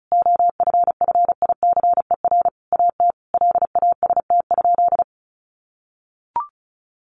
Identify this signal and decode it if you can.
Morse code, heard as a keyed tone, about 34 wpm, 700 Hz. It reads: OFFICER AT LAST?
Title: Sentences from Top 1300 Words 35wpm